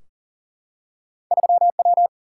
Morse code, heard as a keyed tone, about 42 wpm, 700 Hz. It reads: 3W
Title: Single-Letter Single-Number 40wpm